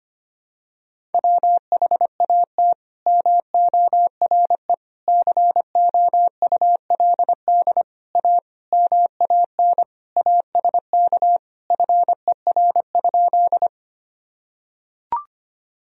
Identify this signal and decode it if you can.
Morse code, heard as a keyed tone, about 25 wpm, 700 Hz. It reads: WHAT MORE COULD A MAN ASK FER?
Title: Sentences from Top 200 Words and Abbreviations 25wpm